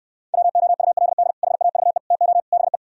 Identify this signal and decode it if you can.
Morse code, heard as a keyed tone, about 68 wpm, 700 Hz. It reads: WOULD HAVE TO BE